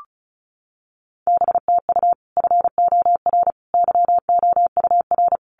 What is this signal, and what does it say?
Morse code, heard as a keyed tone, about 34 wpm, 700 Hz. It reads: BTU FOR YOUR
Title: Common QSO Phrases 35wpm